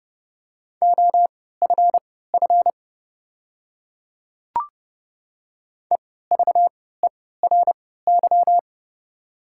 Morse code, heard as a keyed tone, about 30 wpm, spaced as for 17 wpm, 700 Hz. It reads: OFF EVERY
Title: 3x Character Spacing - Top 200 Words - 30wpm